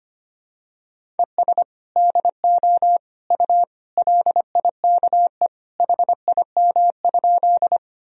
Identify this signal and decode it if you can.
Morse code, heard as a keyed tone, about 25 wpm, 700 Hz. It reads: ES DO U LIKE HIM?
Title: Sentences from Top 100 Words and Abbreviations 25wpm